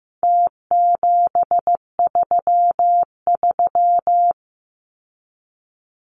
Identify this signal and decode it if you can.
Morse code, heard as a keyed tone, about 15 wpm, 700 Hz. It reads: T733
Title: Numbers - Contest Style - 4-Digit 15wpm